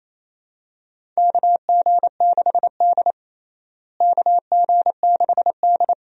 Morse code, heard as a keyed tone, about 28 wpm, 700 Hz. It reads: KG6D KG6D
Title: USA Call Signs - 4 Characters - Repeated 2x - 28wpm